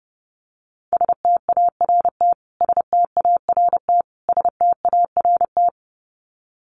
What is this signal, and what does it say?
Morse code, heard as a keyed tone, about 30 wpm, 700 Hz. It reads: START START START
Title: Top 500 Words Repeated 30wpm